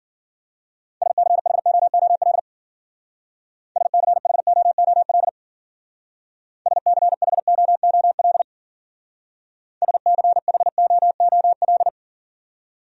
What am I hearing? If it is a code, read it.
Morse code, heard as a keyed tone, about 50 wpm, 700 Hz. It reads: SCHOOL SCHOOL SCHOOL SCHOOL
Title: Speed-Racing Top 500 Words 40wpm